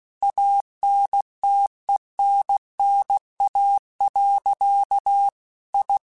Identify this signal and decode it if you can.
Morse code, heard as a keyed tone, about 16 wpm, 800 Hz. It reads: ANTENNA. I
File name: QSOs/QSOs-mp3s/kc7haa@15